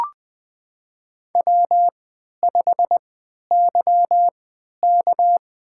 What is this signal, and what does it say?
Morse code, heard as a keyed tone, about 20 wpm, spaced as for 11 wpm, 700 Hz. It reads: W5YK
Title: USA Call Signs - 4 Characters - 3x Character Spacing - 20wpm